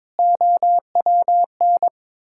Morse code, heard as a keyed tone, about 22 wpm, 700 Hz. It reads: OWN
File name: practice / 22wpm / Sets of 8 Words - 22wpm